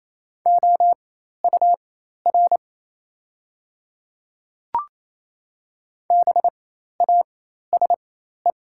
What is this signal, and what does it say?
Morse code, heard as a keyed tone, about 28 wpm, spaced as for 13 wpm, 700 Hz. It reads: OUR BASE
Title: 4x Character Spacing - Top 500 Words - 28wpm